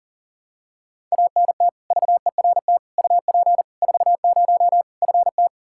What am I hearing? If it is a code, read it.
Morse code, heard as a keyed tone, about 40 wpm, 700 Hz. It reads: ANT VERT UP 40 FT
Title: Practice QSO - Exchange 2 - Info - 40wpm